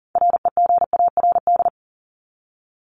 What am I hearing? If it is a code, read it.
Morse code, heard as a keyed tone, about 40 wpm, 700 Hz. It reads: REGARD
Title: ED Suffix - Top 100 Words with Base Words 40wpm